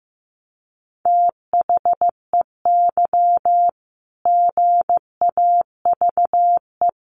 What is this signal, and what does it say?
Morse code, heard as a keyed tone, about 15 wpm, 700 Hz. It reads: THEY GAVE
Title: Sentences from Top 1500 Words 15wpm